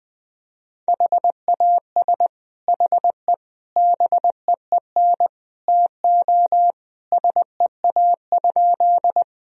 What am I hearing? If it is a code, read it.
Morse code, heard as a keyed tone, about 20 wpm, 700 Hz. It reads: HAS HE BEEN TO SEA?
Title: Sentences from Top 1000 Words - Review Part 1 - 20wpm